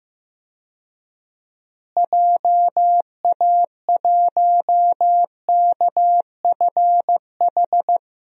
Morse code, heard as a keyed tone, about 15 wpm, 700 Hz. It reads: JA1KFH
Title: DXCC-100 Call Signs - 6 Characters - Repeated 3x - 15wpm